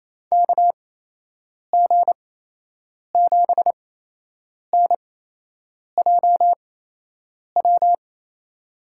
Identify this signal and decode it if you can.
Morse code, heard as a keyed tone, about 28 wpm, spaced as for 8 wpm, 700 Hz. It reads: KG7NJW
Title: USA Call Signs - 6 Characters - 8x Character Spacing - 28wpm